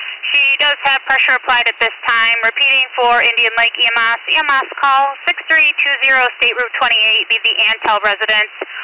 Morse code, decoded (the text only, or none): none